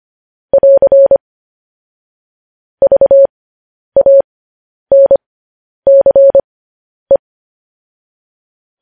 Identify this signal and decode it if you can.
Morse code, heard as a keyed tone, about 25 wpm, spaced as for 10 wpm, 550 Hz. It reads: <AR> VANCE
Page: news-headlines-in-morse-code-at-10-wpm